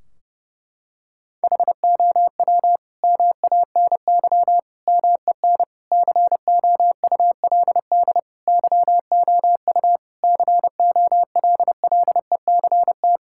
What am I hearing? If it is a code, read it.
Morse code, heard as a keyed tone, about 30 wpm, 700 Hz. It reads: HOW MANY MEN COULD YOU COLLECT
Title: Sentences from Top 1600 Words 30wpm